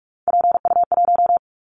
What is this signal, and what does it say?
Morse code, heard as a keyed tone, about 46 wpm, 700 Hz. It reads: PU1
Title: Callsign Prefixes for Top 50 DXCC - 45wpm